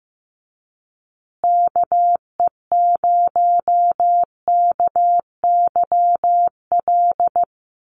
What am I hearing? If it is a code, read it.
Morse code, heard as a keyed tone, about 15 wpm, 700 Hz. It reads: KE0KYL
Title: USA Call Signs Repeated 3x 15wpm